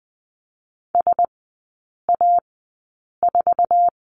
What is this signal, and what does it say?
Morse code, heard as a keyed tone, about 20 wpm, spaced as for 8 wpm, 700 Hz. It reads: SA4
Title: 047 Lesson 9e - Number 4 - Three Characters 20wpm